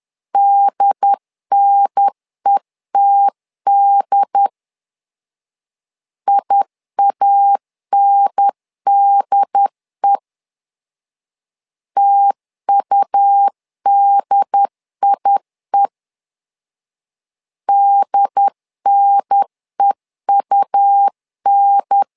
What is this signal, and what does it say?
Morse code, heard as a keyed tone, about 10 wpm, 800 Hz. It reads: DNETD IANDE TUDIE DNEUN
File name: LEZIONI / 2 UHDN / 2 UHDN 10wpm 1904